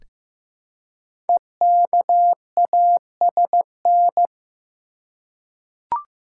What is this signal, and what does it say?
Morse code, heard as a keyed tone, about 15 wpm, 700 Hz. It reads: EKASN
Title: Code Groups - 5 Letters - 15wpm